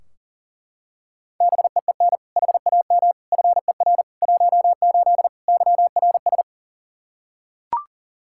Morse code, heard as a keyed tone, about 40 wpm, 700 Hz. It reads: BEEN HAM FER 18 YRS
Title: Practice QSO - Exchange 3 - More Info - 40wpm